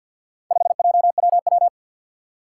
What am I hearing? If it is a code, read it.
Morse code, heard as a keyed tone, about 47 wpm, 700 Hz. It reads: 5JWW
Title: Four Challenging Letter-Number Combinations 50wpm